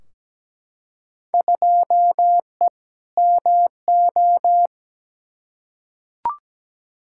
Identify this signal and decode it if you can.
Morse code, heard as a keyed tone, about 17 wpm, 700 Hz. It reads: 2E MO
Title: Field Day Exchanges - 17wpm